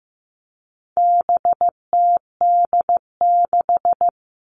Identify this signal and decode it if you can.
Morse code, heard as a keyed tone, about 15 wpm, 700 Hz. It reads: BTD6
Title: All Cut Numbers - 4-Digit 15wpm